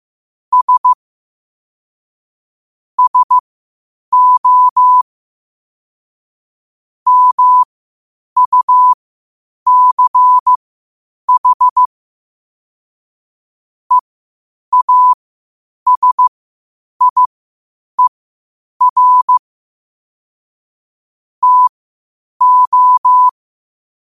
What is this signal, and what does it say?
Morse code, heard as a keyed tone, about 15 wpm, spaced as for 8 wpm, 1000 Hz. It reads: S SO MUCH EASIER TO
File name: qotdfiles / 8wpm